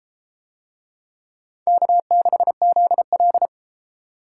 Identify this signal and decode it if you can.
Morse code, heard as a keyed tone, about 31 wpm, 700 Hz. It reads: K6ZL